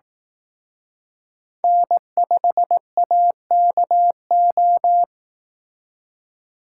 Morse code, heard as a keyed tone, about 18 wpm, 700 Hz. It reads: N5AKO
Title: Speed Racing - USA 1x3 Call Signs 15wpm